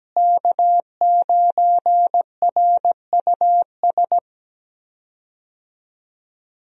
Morse code, heard as a keyed tone, about 17 wpm, 700 Hz. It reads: K9RUS
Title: USA Call Signs - 5 Characters - Repeated 3x - 17wpm